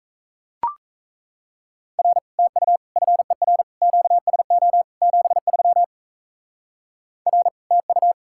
Morse code, heard as a keyed tone, about 37 wpm, 700 Hz. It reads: R TU FER QSO 73 R TU